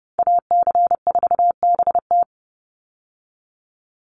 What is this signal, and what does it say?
Morse code, heard as a keyed tone, about 30 wpm, 700 Hz. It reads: AC4BT
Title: Call Signs - 30wpm